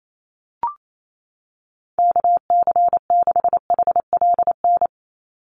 Morse code, heard as a keyed tone, about 28 wpm, 700 Hz. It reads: KC6HLN
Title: USA Call Signs 28wpm